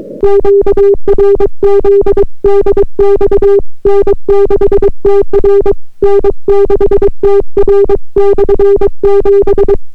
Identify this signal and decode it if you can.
Morse code, heard as a keyed tone, about 23 wpm, 400 Hz. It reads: QRZ DX N6TR N6TR/7